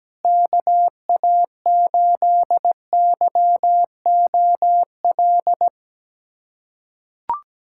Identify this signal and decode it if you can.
Morse code, heard as a keyed tone, about 17 wpm, 700 Hz. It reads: KA8YOL
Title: USA Callsign DE Callsign - 6 Characters - 17wpm